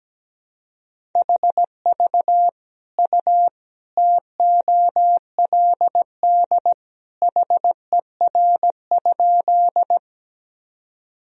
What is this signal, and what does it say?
Morse code, heard as a keyed tone, about 17 wpm, 700 Hz. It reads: HV U TOLD HER?
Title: Sentences from Top 400 Words and Abbreviations 17wpm